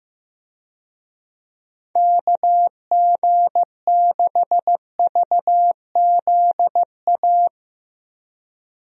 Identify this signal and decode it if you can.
Morse code, heard as a keyed tone, about 15 wpm, 700 Hz. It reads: KG6VZA
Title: USA Call Signs - 6 Characters - Repeated 3x - 15wpm